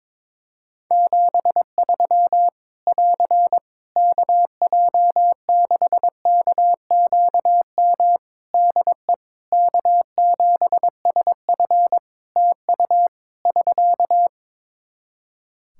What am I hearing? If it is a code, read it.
Morse code, heard as a keyed tone, about 22 wpm, 700 Hz. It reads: 73 <AR> KJ6KQM DE K7SF TU <SK>
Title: Practice QSO - Exchange 4 - Ending - 22wpm